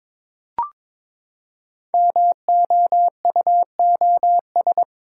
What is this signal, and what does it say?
Morse code, heard as a keyed tone, about 22 wpm, 700 Hz. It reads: MOUOS
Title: Code Groups - 5 Letters - 22wpm